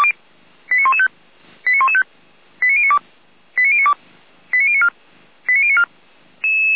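Five-tone selective call (ZVEI), heard as quick runs of five tones, tones of about 80 ms. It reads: 89115 89115 89092 89092 89094 89094
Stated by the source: Five-tone ZVEI selective call received on an SDR.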